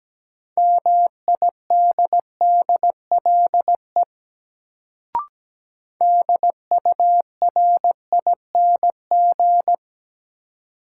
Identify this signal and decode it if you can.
Morse code, heard as a keyed tone, about 17 wpm, 700 Hz. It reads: MIDDLE DURING
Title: Sets of 2 Words - New Word Drill - 17wpm